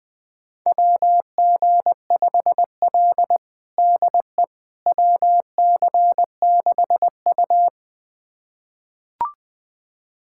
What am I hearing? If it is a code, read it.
Morse code, heard as a keyed tone, about 20 wpm, 700 Hz. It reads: WG5L DE WC6U
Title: USA Callsign DE Callsign - 4 Characters - 20wpm